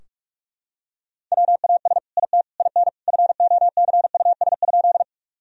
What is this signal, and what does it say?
Morse code, heard as a keyed tone, about 46 wpm, 700 Hz. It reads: WAS IT IN FOCUS?
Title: Sentences from Top 1000 Words - Review Part 4 - 45wpm